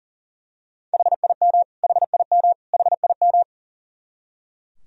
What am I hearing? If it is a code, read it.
Morse code, heard as a keyed tone, about 40 wpm, 700 Hz. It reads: HIM HIM HIM
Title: Top 100 Words Repeated 40wpm